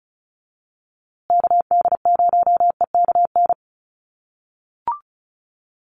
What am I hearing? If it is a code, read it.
Morse code, heard as a keyed tone, about 34 wpm, 700 Hz. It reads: KD0EKN
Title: USA Call Signs Repeated 3x 35wpm